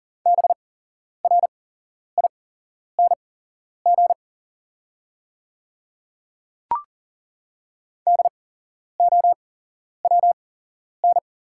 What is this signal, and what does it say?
Morse code, heard as a keyed tone, about 40 wpm, spaced as for 11 wpm, 690 Hz. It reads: BRING DOWN
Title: 8x Character Spacing - Top 500 Words - 40wpm